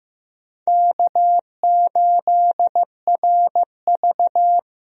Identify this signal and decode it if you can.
Morse code, heard as a keyed tone, about 15 wpm, 700 Hz. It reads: K8RV
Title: USA Call Signs - 4 Characters - Repeated 2x - 15wpm